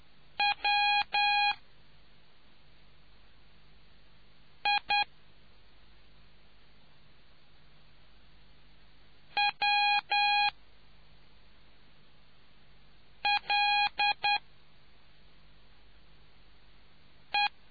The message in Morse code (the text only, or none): W I W L E